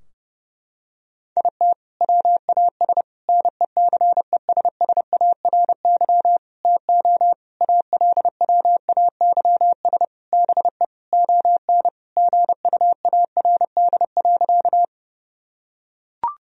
Morse code, heard as a keyed tone, about 30 wpm, 700 Hz. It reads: IT WAS NECESSARY TO ALWAYS BE ON GUARD.
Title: Sentences from Top 1000 Words - Review Part 4 - 30wpm